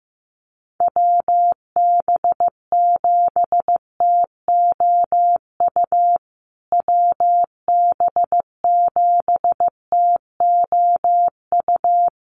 Morse code, heard as a keyed tone, about 15 wpm, 700 Hz. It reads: WB7TOU WB7TOU
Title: USA Call Signs Repeated 2x 15wpm